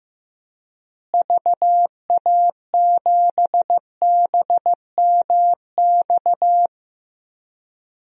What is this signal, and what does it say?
Morse code, heard as a keyed tone, about 15 wpm, 700 Hz. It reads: VA7BMX
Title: DXCC-100 Call Signs - 6 Characters - Repeated 3x - 15wpm